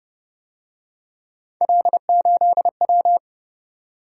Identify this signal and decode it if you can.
Morse code, heard as a keyed tone, about 32 wpm, 700 Hz. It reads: L8W